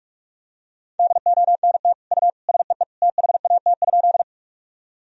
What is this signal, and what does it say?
Morse code, heard as a keyed tone, about 46 wpm, 700 Hz. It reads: DONT U SEE THAT?
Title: Sentences from Top 300 Words and Abbreviations 45wpm